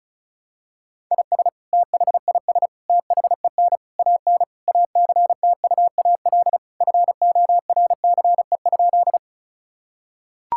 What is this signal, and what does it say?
Morse code, heard as a keyed tone, about 34 wpm, 700 Hz. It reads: IS THIS THEN AN ACTUAL FORCE?
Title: Sentences from Top 1000 Words - Review Part 5 - 35wpm